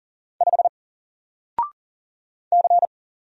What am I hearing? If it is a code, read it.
Morse code, heard as a keyed tone, about 40 wpm, 700 Hz. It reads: 5 C
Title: Single Character (Letter:Number:Punctuation:Prosign) 40wpm